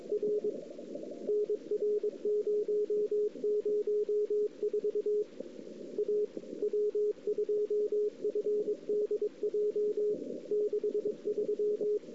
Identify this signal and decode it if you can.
Morse code, heard as a keyed tone, about 25 wpm, 410 Hz. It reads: R NR 004 A W2FDJ 63
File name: CW / 2008-ss-cw-k5tr-op-n5rz -02:00